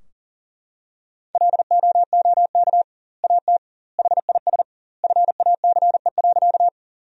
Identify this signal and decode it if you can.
Morse code, heard as a keyed tone, about 40 wpm, 700 Hz. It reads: LOOK AT HIS FACE.
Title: 2x Word-Spacing - Sentences from Top 300 Words - 40wpm